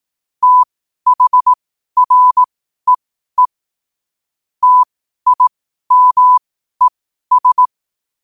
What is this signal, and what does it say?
Morse code, heard as a keyed tone, about 18 wpm, spaced as for 12 wpm, 1000 Hz. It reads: THREE TIMES
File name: qotdfiles / 12wpm